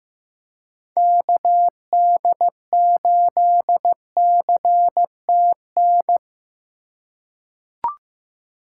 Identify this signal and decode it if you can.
Morse code, heard as a keyed tone, about 15 wpm, 700 Hz. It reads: KD8CTN
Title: USA Call Signs - 2x Character Spacing - 15wpm